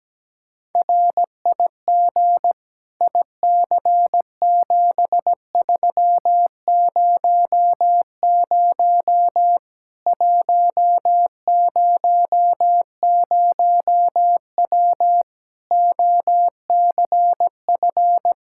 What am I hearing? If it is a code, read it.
Morse code, heard as a keyed tone, about 17 wpm, 700 Hz. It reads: RIG IC7300 100W OCF
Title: Practice QSO - Exchange 2 - Info - 17wpm